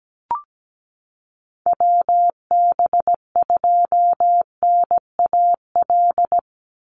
Morse code, heard as a keyed tone, about 17 wpm, 700 Hz. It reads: WB2NAL
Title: USA Call Signs 17wpm